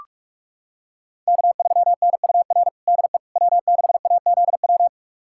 Morse code, heard as a keyed tone, about 42 wpm, 700 Hz. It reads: K3NUR DE W6AZW
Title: USA Callsign DE Callsign - 5 Characters - 45wpm